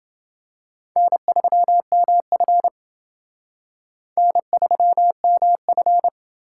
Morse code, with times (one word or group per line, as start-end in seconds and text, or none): 1.0-2.7 N3MF
4.2-6.1 N3MF